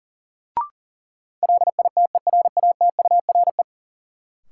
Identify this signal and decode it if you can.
Morse code, heard as a keyed tone, about 40 wpm, 700 Hz. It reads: LITERATURE
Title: Sentences from Top 600 Words - New Word Drill - 40wpm